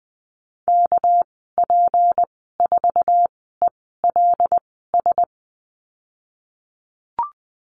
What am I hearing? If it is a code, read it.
Morse code, heard as a keyed tone, about 20 wpm, spaced as for 14 wpm, 700 Hz. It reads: KP4ELS
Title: 245 Lesson 39g - Letter z - Call Signs 20wpm